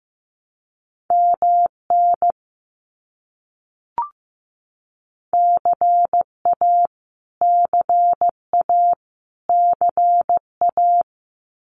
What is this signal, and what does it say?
Morse code, heard as a keyed tone, about 15 wpm, 700 Hz. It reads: MN CA CA CA
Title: US States Repeated 3x 15wpm